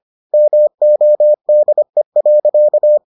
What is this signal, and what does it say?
Morse code, heard as a keyed tone, about 25 wpm, 600 Hz. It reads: MODE.